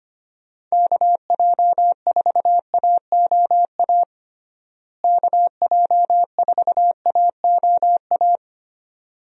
Morse code, heard as a keyed tone, about 25 wpm, 700 Hz. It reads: KJ4AOA KJ4AOA
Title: USA Call Signs - 6 Characters - Repeated 2x - 25wpm